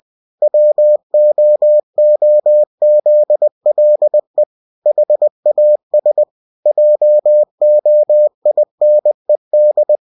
Morse code, heard as a keyed tone, about 20 wpm, 600 Hz. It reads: WOOZLE HAS JOINED
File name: cw-books / winnie-the-pooh / winnie-20wpm-page-0023